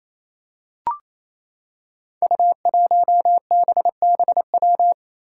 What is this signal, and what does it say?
Morse code, heard as a keyed tone, about 29 wpm, 700 Hz. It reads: U1BBW
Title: Five Challenging Letter-Number Combinations 28wpm